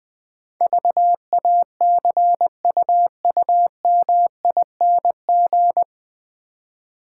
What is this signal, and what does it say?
Morse code, heard as a keyed tone, about 20 wpm, 700 Hz. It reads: VACUUMING
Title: Easy Words - Speed-Racing with Hints - 20wpm